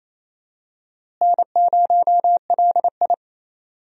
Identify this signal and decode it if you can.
Morse code, heard as a keyed tone, about 28 wpm, 700 Hz. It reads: N0LI
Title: USA Call Signs - 4 Characters - Repeated 3x - 28wpm